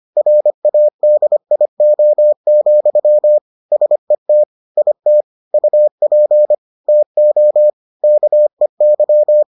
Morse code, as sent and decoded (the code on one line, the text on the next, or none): .-. .- -.. .. --- --..-- / ... - / .. - / ..- .--. / - --- / -.- . -.--
RADIO, SET IT UP TO KEY